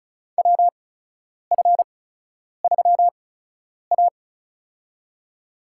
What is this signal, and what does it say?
Morse code, heard as a keyed tone, about 36 wpm, spaced as for 10 wpm, 700 Hz. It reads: WF3A
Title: USA Call Signs - 4 Characters - 8x Character Spacing - 35wpm